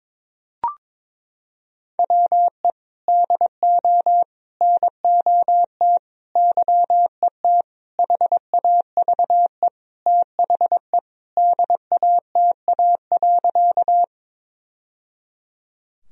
Morse code, heard as a keyed tone, about 22 wpm, 700 Hz. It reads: WE DO NOT YET HAVE THE DATA.
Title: Sentences from Top 1000 Words - Review Part 3 - 22wpm